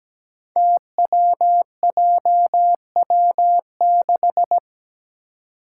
Morse code, heard as a keyed tone, about 17 wpm, 700 Hz. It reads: TWJW6
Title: Five Challenging Letter-Number Combinations 17wpm